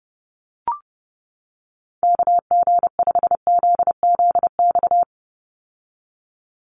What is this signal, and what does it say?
Morse code, heard as a keyed tone, about 30 wpm, 700 Hz. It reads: KG5ZZX
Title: USA Call Signs 30wpm